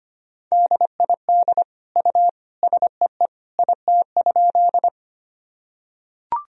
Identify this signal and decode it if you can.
Morse code, heard as a keyed tone, about 25 wpm, 700 Hz. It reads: DID U SEE IT?
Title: Sentences from Top 100 Words and Abbreviations 25wpm